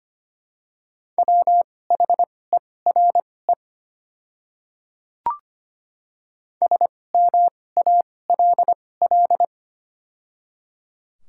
Morse code, heard as a keyed tone, about 25 wpm, spaced as for 18 wpm, 690 Hz. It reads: WHERE SMALL
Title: 2x Character Spacing - Top 200 Words - 25wpm